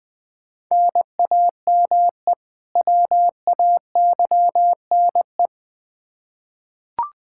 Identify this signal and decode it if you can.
Morse code, heard as a keyed tone, about 20 wpm, 700 Hz. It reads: NAME WAYNE
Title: Practice QSO - Exchange 1 - Beginning - 20wpm